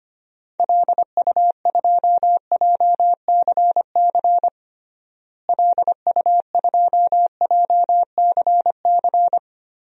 Morse code, heard as a keyed tone, about 25 wpm, 700 Hz. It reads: LU2JCC LU2JCC
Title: DXCC-100 Call Signs - 6 Characters - Repeated 2x - 25wpm